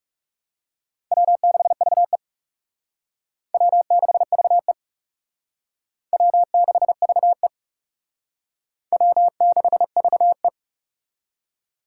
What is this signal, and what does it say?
Morse code, heard as a keyed tone, about 38 wpm, 700 Hz. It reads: W6VE W6VE W6VE W6VE
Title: Speed Racing - USA 1x2 Call Signs 30wpm